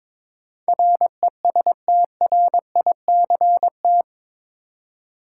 Morse code, heard as a keyed tone, about 21 wpm, 700 Hz. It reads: RESTRICT
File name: practice / 22wpm / Eight Letter Words - 22wpm